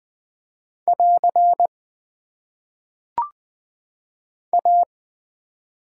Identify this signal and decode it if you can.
Morse code, heard as a keyed tone, about 20 wpm, 700 Hz. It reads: <AR> A
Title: Top 90 QSO Elements - 6x Character Spacing - 20wpm